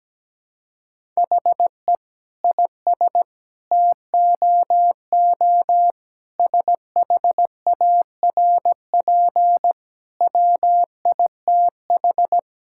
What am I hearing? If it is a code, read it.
Morse code, heard as a keyed tone, about 17 wpm, 700 Hz. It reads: HE IS TOO SHARP WITH
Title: Sentences from Top 1600 Words 17wpm